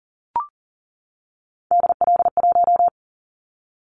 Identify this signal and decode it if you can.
Morse code, heard as a keyed tone, about 42 wpm, 700 Hz. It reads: DL1
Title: Callsign Prefixes for Top 25 DXCC - 40wpm